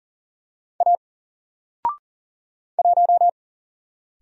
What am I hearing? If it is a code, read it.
Morse code, heard as a keyed tone, about 39 wpm, 700 Hz. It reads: A 1
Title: Single Character (Letter:Number:Punctuation:Prosign) 40wpm